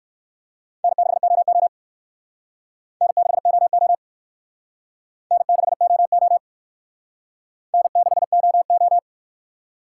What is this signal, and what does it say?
Morse code, heard as a keyed tone, about 55 wpm, 700 Hz. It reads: N6OO N6OO N6OO N6OO
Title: Speed Racing - USA 1x2 Call Signs 45wpm